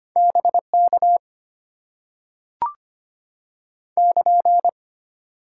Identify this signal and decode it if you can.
Morse code, heard as a keyed tone, about 25 wpm, 690 Hz. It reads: BK <KN>
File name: Top 90 QSO Elements - 8x Character Spacing - 25wpm